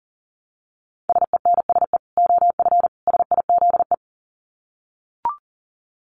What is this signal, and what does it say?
Morse code, heard as a keyed tone, about 40 wpm, 700 Hz. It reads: SENSE OF SIZE
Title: Dit Challenge 40wpm